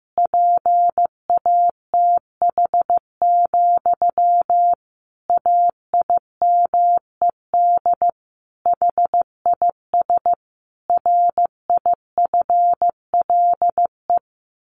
Morse code, heard as a keyed tone, about 15 wpm, 700 Hz. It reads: PATH, AIMED HIS RIFLE
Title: A Sound Of Thunder 15wpm